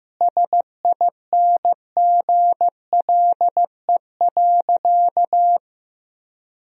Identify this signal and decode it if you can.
Morse code, heard as a keyed tone, about 16 wpm, 700 Hz. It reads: SINGLE.
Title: Sentences from Top 1000 Words - Review Part 4 - 15wpm